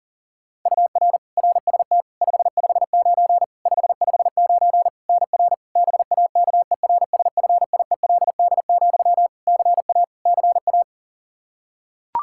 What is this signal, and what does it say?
Morse code, heard as a keyed tone, about 40 wpm, 700 Hz. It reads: UR RST 559 559 NR BAKERSFIELD, CA CA
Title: Practice QSO - Exchange 1 - Beginning - 40wpm